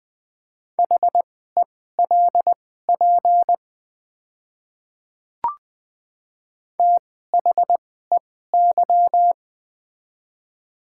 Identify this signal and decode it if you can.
Morse code, heard as a keyed tone, about 20 wpm, spaced as for 14 wpm, 700 Hz. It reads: HELP THEY
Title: 2x Character Spacing - Top 300 Words - 20wpm